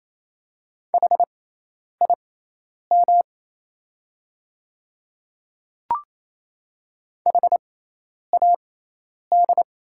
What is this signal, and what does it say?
Morse code, heard as a keyed tone, about 28 wpm, spaced as for 10 wpm, 700 Hz. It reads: HIM HAD
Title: 6x Character Spacing - Top 100 Words - 28wpm